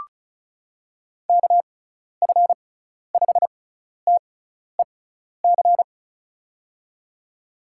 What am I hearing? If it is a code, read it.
Morse code, heard as a keyed tone, about 36 wpm, spaced as for 12 wpm, 690 Hz. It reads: KF5TEC
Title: USA Call Signs - 6 Characters - 6x Character Spacing - 35wpm